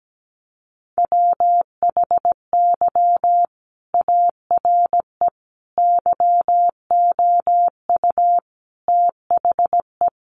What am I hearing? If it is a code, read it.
Morse code, heard as a keyed tone, about 17 wpm, 700 Hz. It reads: WHY ARE YOU THE
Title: Sentences from Top 1200 Words 17wpm